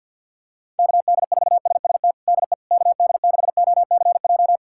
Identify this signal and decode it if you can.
Morse code, heard as a keyed tone, about 52 wpm, 700 Hz. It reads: KD4SST DE KD6OCJ
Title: USA Callsign DE Callsign - 6 Characters - 50wpm